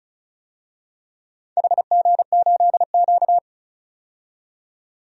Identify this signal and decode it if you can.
Morse code, heard as a keyed tone, about 36 wpm, 700 Hz. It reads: HG8Q
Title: DXCC-100 Call Signs - 4 Characters - 35wpm